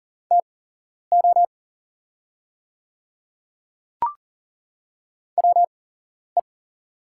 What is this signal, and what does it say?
Morse code, heard as a keyed tone, about 38 wpm, spaced as for 11 wpm, 700 Hz. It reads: TO WE
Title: 8x Character Spacing - Top 100 Words - 40wpm